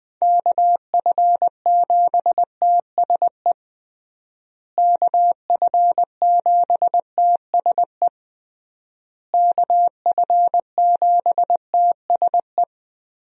Morse code, heard as a keyed tone, about 21 wpm, 700 Hz. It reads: KF7TSE KF7TSE KF7TSE
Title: USA Call Signs - 6 Characters - Repeated 3x - 20wpm